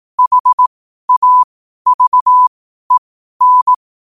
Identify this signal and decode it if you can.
Morse code, heard as a keyed tone, about 18 wpm, spaced as for 12 wpm, 1000 Hz. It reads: HAVEN
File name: qotdfiles / 12wpm